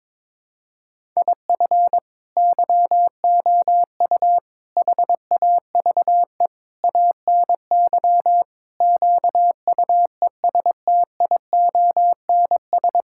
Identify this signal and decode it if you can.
Morse code, heard as a keyed tone, about 22 wpm, 700 Hz. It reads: IF YOU HAVE ANY QUESTIONS
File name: practice / 22wpm / Sentences with 8-Word Phrases - 22wpm